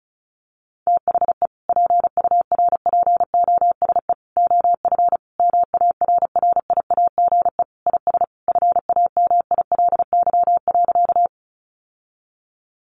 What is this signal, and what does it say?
Morse code, heard as a keyed tone, about 34 wpm, 700 Hz. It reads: THE PURPOSE OF MARRIAGE IS FAMILY.
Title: Sentences from Top 800 Words 35wpm